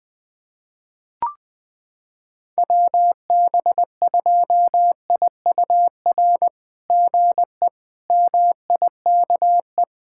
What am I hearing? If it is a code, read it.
Morse code, heard as a keyed tone, about 20 wpm, 690 Hz. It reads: WB2IUR GE MIKE
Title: SOTA QSO Practice 20wpm